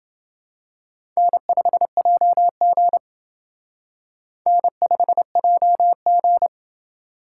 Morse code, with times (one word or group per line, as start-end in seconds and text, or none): 1.2-3.0 N5JG
4.5-6.5 N5JG